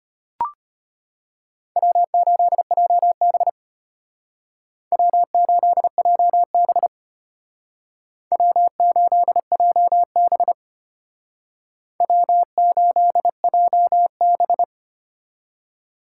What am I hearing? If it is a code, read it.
Morse code, heard as a keyed tone, about 33 wpm, 700 Hz. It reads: W8JB W8JB W8JB W8JB